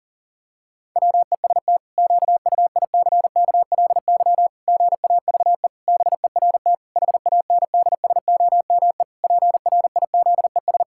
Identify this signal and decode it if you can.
Morse code, heard as a keyed tone, about 37 wpm, 700 Hz. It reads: WEST QUICKLY GAVE BERT HANDSOME PRIZES
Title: Pangram Challenge v2 - 40wpm